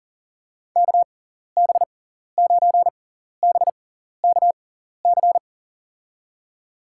Morse code, heard as a keyed tone, about 37 wpm, spaced as for 14 wpm, 700 Hz. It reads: KB9BKC
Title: USA Call Signs - 6 Characters - 6x Character Spacing - 40wpm